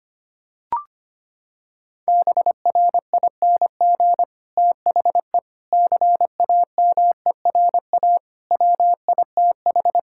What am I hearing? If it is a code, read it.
Morse code, heard as a keyed tone, about 25 wpm, 700 Hz. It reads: BRING THE CAMERA WITH
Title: Sentences from Top 1000 Words - Review Part 5 - 25wpm